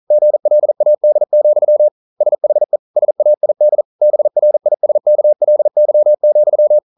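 Morse code, heard as a keyed tone, about 42 wpm, 600 Hz. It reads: GLAD, SHE SAID BRISKLY,
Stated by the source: Morse code practice at 41 wpm, from CW med Gnister.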